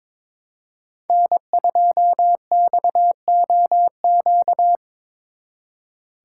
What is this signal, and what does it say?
Morse code, heard as a keyed tone, about 25 wpm, 700 Hz. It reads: N2XOQ
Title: Speed Racing - USA Call Signs 22wpm